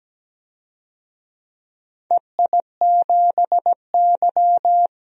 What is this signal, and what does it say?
Morse code, heard as a keyed tone, about 17 wpm, 700 Hz. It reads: EI7Y
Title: DXCC-100 Call Signs - 4 Characters - 17wpm